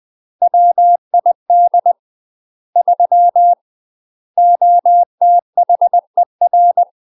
Morse code, heard as a keyed tone, about 20 wpm, 700 Hz. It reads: WID 3 OTHER
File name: UK MASTER QSOs TRAINING / 20WPM / UK MASTER QSOs 0013